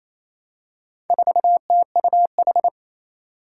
Morse code, heard as a keyed tone, about 27 wpm, 700 Hz. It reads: 4TUH
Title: Four Challenging Letter-Number Combinations 28wpm